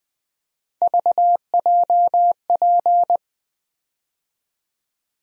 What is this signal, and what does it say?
Morse code, heard as a keyed tone, about 21 wpm, 700 Hz. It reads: VJP
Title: USA Call Sign Postfixes - 3 Letters 20wpm